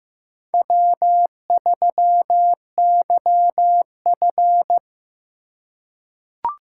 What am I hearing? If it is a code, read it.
Morse code, heard as a keyed tone, about 15 wpm, 700 Hz. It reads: W3YF
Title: USA Call Signs - 4 Characters - 3x Character Spacing - 15wpm